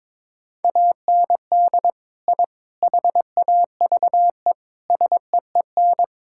Morse code, heard as a keyed tone, about 22 wpm, 700 Hz. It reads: AND I HAVE SEEN
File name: practice / 22wpm / Sentences with 3-Word Phrases - 22wpm